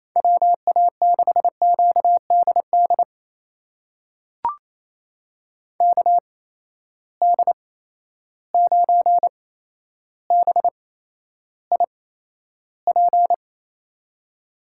Morse code, heard as a keyed tone, about 28 wpm, spaced as for 12 wpm, 700 Hz. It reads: WA6QDD KD9BIP
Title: USA Call Signs - 6 Characters - 8x Character Spacing - 28wpm